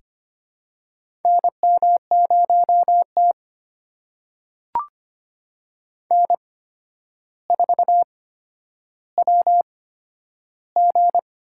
Morse code, heard as a keyed tone, about 25 wpm, spaced as for 11 wpm, 700 Hz. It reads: NM0T N4WG
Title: USA Call Signs - 4 Characters - 8x Character Spacing - 25wpm